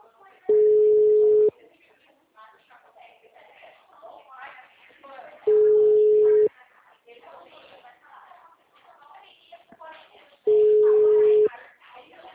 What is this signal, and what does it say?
Ringback tone (Europe, 425 Hz), with speech.
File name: out- -2007- - - .25930